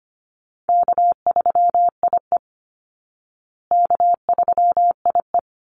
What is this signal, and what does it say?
Morse code, heard as a keyed tone, about 25 wpm, 700 Hz. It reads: K3IE K3IE
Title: Call Signs Repeated 3x - 25wpm